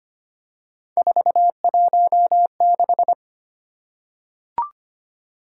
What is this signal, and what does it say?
Morse code, heard as a keyed tone, about 25 wpm, 700 Hz. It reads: 416
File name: Three Letter-Number Combinations - 25wpm